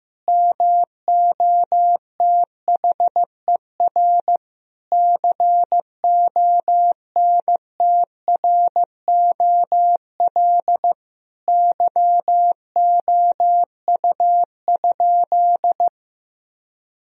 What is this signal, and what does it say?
Morse code, heard as a keyed tone, about 15 wpm, 700 Hz. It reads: MOTHER CONTROL YOU?
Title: Sentences from Top 1000 Words - Review Part 3 - 15wpm